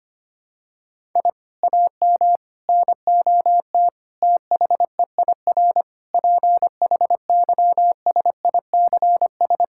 Morse code, heard as a keyed tone, about 25 wpm, 700 Hz. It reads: I AM NOT THEIR PHYSICS
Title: Sentences from Top 1000 Words - Review Part 6 - 25wpm